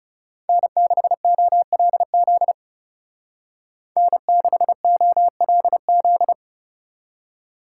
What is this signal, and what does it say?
Morse code, heard as a keyed tone, about 37 wpm, 700 Hz. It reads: N6OLZ N6OLZ
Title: Speed Racing - USA 1x3 Call Signs 30wpm